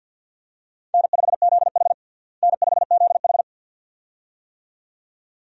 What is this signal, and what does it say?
Morse code, heard as a keyed tone, about 49 wpm, 700 Hz. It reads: N5ZH N5ZH
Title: USA Call Signs - 4 Characters - Repeated 2x - 50wpm